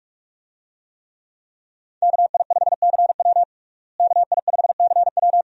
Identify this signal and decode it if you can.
Morse code, heard as a keyed tone, about 42 wpm, 700 Hz. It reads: KI5CW KI5CW
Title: USA Call Signs - 5 Characters - Repeated 2x - 45wpm